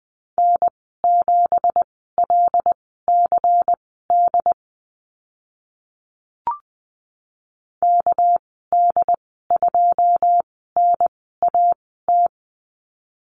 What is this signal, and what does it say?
Morse code, heard as a keyed tone, about 19 wpm, spaced as for 14 wpm, 700 Hz. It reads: N7LCD KD2NAT
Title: 245 Lesson 39g - Letter z - Call Signs 20wpm